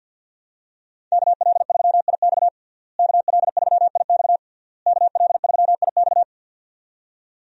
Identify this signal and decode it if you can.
Morse code, heard as a keyed tone, about 50 wpm, 700 Hz. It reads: KL3IX KL3IX KL3IX
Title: USA Call Signs - 5 Characters - Repeated 3x - 50wpm